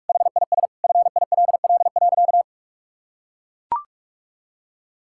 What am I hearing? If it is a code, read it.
Morse code, heard as a keyed tone, about 41 wpm, 700 Hz. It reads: HIS FILL.
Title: Sentences from Top 1000 Words - Review Part 5 - 45wpm